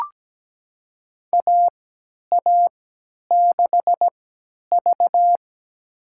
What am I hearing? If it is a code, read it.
Morse code, heard as a keyed tone, about 17 wpm, spaced as for 10 wpm, 700 Hz. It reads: AA6V
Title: USA Call Signs - 4 Characters - 3x Character Spacing - 17wpm